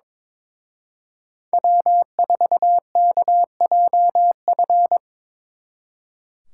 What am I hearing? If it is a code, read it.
Morse code, heard as a keyed tone, about 25 wpm, 700 Hz. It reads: W4KJF
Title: Speed Racing - USA Call Signs 22wpm